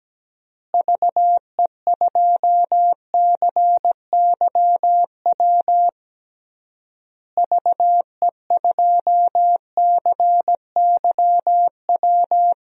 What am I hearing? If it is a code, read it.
Morse code, heard as a keyed tone, about 17 wpm, 700 Hz. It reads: VE2CYW VE2CYW
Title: DXCC-100 Call Signs - 6 Characters - Repeated 3x - 17wpm